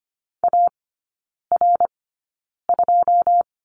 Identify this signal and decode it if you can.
Morse code, heard as a keyed tone, about 25 wpm, spaced as for 9 wpm, 690 Hz. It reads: AR2
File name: Lesson 15e - Number 5 - Three Characters 25wpm